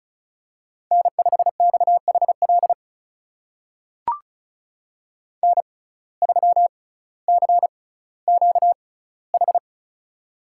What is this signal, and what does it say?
Morse code, heard as a keyed tone, about 36 wpm, spaced as for 18 wpm, 700 Hz. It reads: N5XHL N3CQH
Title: USA Call Signs - 5 Characters - 6x Character Spacing - 35wpm